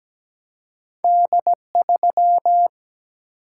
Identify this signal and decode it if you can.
Morse code, heard as a keyed tone, about 17 wpm, 700 Hz. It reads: D3
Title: Single-Letter Single-Number 17wpm